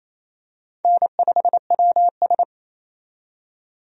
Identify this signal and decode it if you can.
Morse code, heard as a keyed tone, about 27 wpm, 700 Hz. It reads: N5WS
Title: USA Call Signs - 4 Characters - 28wpm